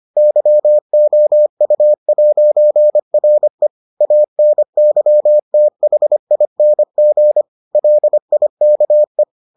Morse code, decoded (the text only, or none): YOU'RE ANYTHING LIKE